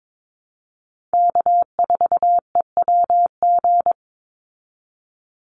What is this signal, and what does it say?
Morse code, heard as a keyed tone, about 22 wpm, 700 Hz. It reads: K4EWG
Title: Call Signs Repeated 3x - 22wpm